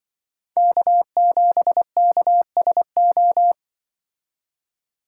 Morse code, heard as a keyed tone, about 23 wpm, 700 Hz. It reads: K7KSO
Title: Speed Racing - USA 1x3 Call Signs 20wpm